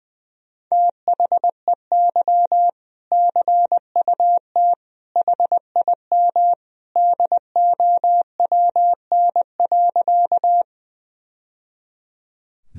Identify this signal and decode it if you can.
Morse code, heard as a keyed tone, about 21 wpm, 700 Hz. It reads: THEY CUT HIM DOWN.
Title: Sentences from Top 1000 Words - Review Part 1 - 20wpm